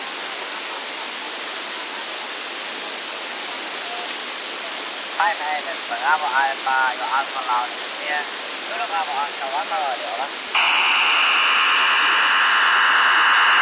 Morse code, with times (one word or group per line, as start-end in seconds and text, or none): none